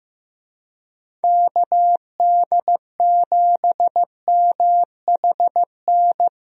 Morse code, heard as a keyed tone, about 15 wpm, 700 Hz. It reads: KD7MHN
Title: USA Call Signs - 6 Characters - 15wpm